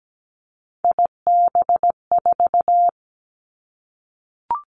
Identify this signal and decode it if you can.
Morse code, heard as a keyed tone, about 17 wpm, 700 Hz. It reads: IB4
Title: Three Challenging Letter-Number Combinations 17wpm